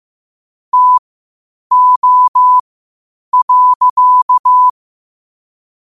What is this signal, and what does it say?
Morse code, heard as a keyed tone, about 15 wpm, spaced as for 8 wpm, 1000 Hz. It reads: TO.
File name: qotdfiles / 8wpm